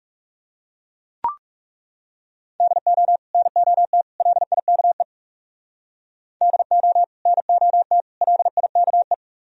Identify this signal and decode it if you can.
Morse code, heard as a keyed tone, about 37 wpm, 690 Hz. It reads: DO NOT LIKE DO NOT LIKE
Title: Speed-Racing - Sets of 3 Words 30wpm